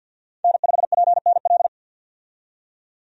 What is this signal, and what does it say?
Morse code, heard as a keyed tone, about 52 wpm, 700 Hz. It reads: N5PNL
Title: USA Call Signs - 5 Characters - 50wpm